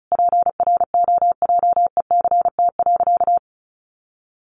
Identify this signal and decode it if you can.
Morse code, heard as a keyed tone, about 34 wpm, 700 Hz. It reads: PROJECT.
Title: Sentences from Top 1400 Words 35wpm